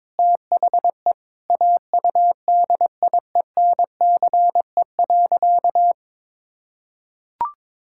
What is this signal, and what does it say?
Morse code, heard as a keyed tone, about 21 wpm, 700 Hz. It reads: THE AUDIENCE.
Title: Sentences from Top 1000 Words - Review Part 5 - 22wpm